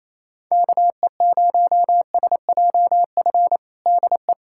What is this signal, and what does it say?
Morse code, heard as a keyed tone, about 28 wpm, 700 Hz. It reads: KE0SJF DE
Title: USA Callsign DE Callsign - 6 Characters - 28wpm